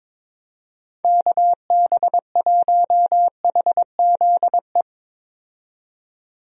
Morse code, heard as a keyed tone, about 22 wpm, 700 Hz. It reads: KB1HZE
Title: USA Call Signs - 6 Characters - 22wpm